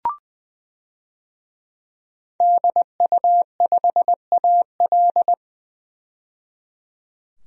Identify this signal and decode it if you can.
Morse code, heard as a keyed tone, about 20 wpm, 700 Hz. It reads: DU5AL
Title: DXCC-100 Call Signs - 5 Characters - 20wpm